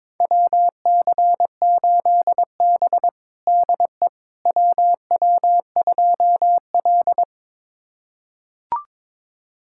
Morse code, heard as a keyed tone, about 21 wpm, 700 Hz. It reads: WC8B DE WW2L
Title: USA Callsign DE Callsign - 4 Characters - 22wpm